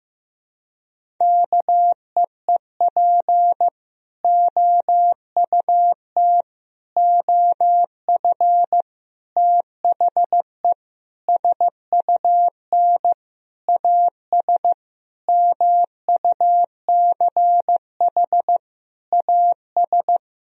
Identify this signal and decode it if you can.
Morse code, heard as a keyed tone, about 15 wpm, 700 Hz. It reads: KEEP OUT OF THE SUN AS MUCH AS
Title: Sentences from Top 1000 Words - Review Part 1 - 15wpm